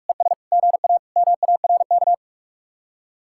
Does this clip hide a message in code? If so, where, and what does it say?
Yes: Morse code from 0.1 to 2.1 s, heard as a keyed tone, about 47 wpm, 700 Hz. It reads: ES GA MARK